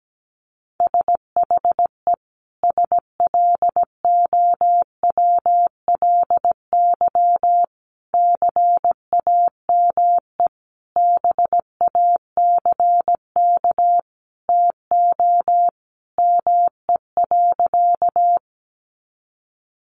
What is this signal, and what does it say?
Morse code, heard as a keyed tone, about 17 wpm, 700 Hz. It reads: SHE SLOWLY CAME BACK TO ME.
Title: Sentences from Top 1100 Words 17wpm